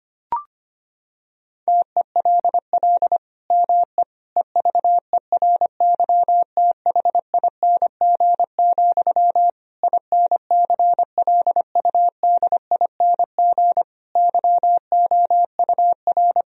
Morse code, heard as a keyed tone, about 25 wpm, 700 Hz. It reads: TELL ME EVERYTHING, INCLUDING YOUR
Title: Sentences from Top 1000 Words - Review Part 4 - 25wpm